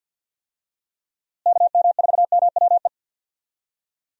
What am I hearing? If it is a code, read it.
Morse code, heard as a keyed tone, about 51 wpm, 700 Hz. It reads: KM4MWE
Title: USA Call Signs - 6 Characters - Repeated 3x - 50wpm